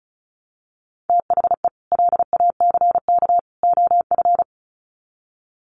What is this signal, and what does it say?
Morse code, heard as a keyed tone, about 34 wpm, 700 Hz. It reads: THE LACK OF
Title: Sets of 3 Words 35wpm